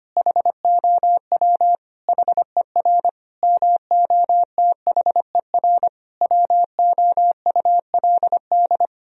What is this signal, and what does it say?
Morse code, heard as a keyed tone, about 25 wpm, 700 Hz. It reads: HOW HER MOTHER WOULD
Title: Sentences from Top 1000 Words - Review Part 4 - 25wpm